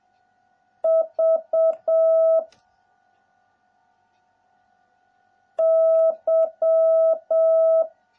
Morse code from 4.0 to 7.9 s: Y